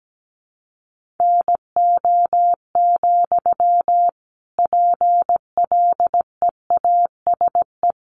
Morse code, heard as a keyed tone, about 17 wpm, 700 Hz. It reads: NO, PLEASE
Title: Sentences from Top 1500 Words 17wpm